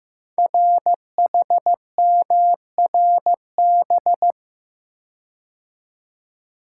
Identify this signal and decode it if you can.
Morse code, heard as a keyed tone, about 15 wpm, 700 Hz. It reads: RHMRB
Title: Code Groups - 5 Letters - 15wpm